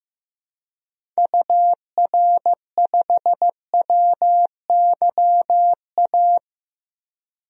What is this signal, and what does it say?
Morse code, heard as a keyed tone, about 15 wpm, 700 Hz. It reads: UR5WYA
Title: DXCC-100 Call Signs - 6 Characters - Repeated 2x - 15wpm